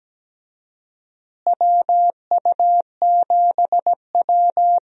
Morse code, heard as a keyed tone, about 17 wpm, 700 Hz. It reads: WU7W
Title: USA Call Signs - 4 Characters - Repeated 3x - 17wpm